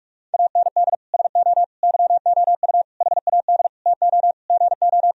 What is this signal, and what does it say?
Morse code, heard as a keyed tone, about 47 wpm, 700 Hz. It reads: AND SO YOU HAD TO GO
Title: Sentences from Top 1000 Words - Review Part 1 - 45wpm